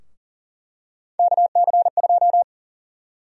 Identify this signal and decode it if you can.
Morse code, heard as a keyed tone, about 40 wpm, 700 Hz. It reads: KC2
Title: USA Call Sign Prefixes - 40wpm